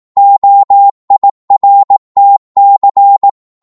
Morse code, heard as a keyed tone, about 18 wpm, 800 Hz. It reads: OIRTC
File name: Morse / ltr-18wpm